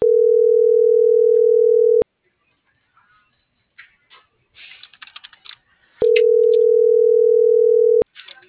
Ringback tone (North America), 440 + 480 Hz.
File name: CENTRAL MABE / out xquenda.trinidad - .74721